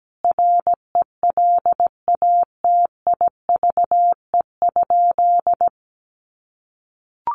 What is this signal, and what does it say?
Morse code, heard as a keyed tone, about 17 wpm, 700 Hz. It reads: RELATIVE?
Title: Sentences from Top 1400 Words 17wpm